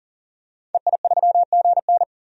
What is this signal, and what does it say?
Morse code, heard as a keyed tone, about 40 wpm, 700 Hz. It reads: EI3GN
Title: DXCC-100 Call Signs - 5 Characters - 40wpm